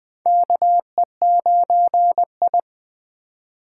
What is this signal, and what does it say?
Morse code, heard as a keyed tone, about 21 wpm, 700 Hz. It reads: KE9I
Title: K1USN SST QSO Practice - 20wpm